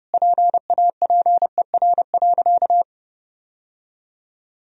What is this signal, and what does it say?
Morse code, heard as a keyed tone, about 32 wpm, 700 Hz. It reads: PAPER.
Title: Sentences from Top 1000 Words - Review Part 2 - 30wpm